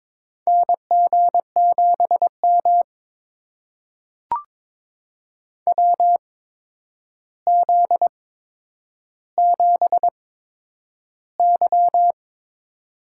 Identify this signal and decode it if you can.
Morse code, heard as a keyed tone, about 22 wpm, spaced as for 9 wpm, 700 Hz. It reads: NG7M WZ7Y
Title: USA Call Signs - 4 Characters - 8x Character Spacing - 22wpm